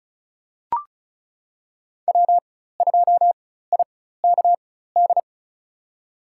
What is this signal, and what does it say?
Morse code, heard as a keyed tone, about 34 wpm, spaced as for 16 wpm, 700 Hz. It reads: W2IKD
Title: USA Call Signs - 5 Characters - 4x Character Spacing - 35wpm